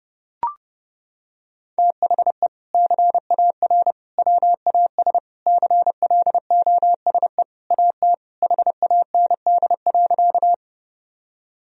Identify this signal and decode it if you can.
Morse code, heard as a keyed tone, about 32 wpm, 700 Hz. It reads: THE CAR WAS CLOSE AT HAND.
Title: Sentences from Top 1000 Words - Review Part 1 - 30wpm